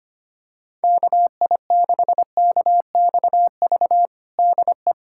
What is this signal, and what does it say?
Morse code, heard as a keyed tone, about 25 wpm, 700 Hz. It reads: KI6KXV DE
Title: USA Callsign DE Callsign 25wpm